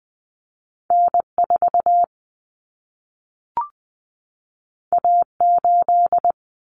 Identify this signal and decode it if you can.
Morse code, heard as a keyed tone, about 20 wpm, 700 Hz. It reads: N4 A8
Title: Common Cut Numbers - 2-Digit 20wpm